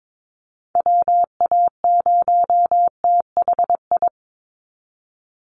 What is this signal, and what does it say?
Morse code, heard as a keyed tone, about 22 wpm, 700 Hz. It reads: WA0THI
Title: USA Call Signs 22wpm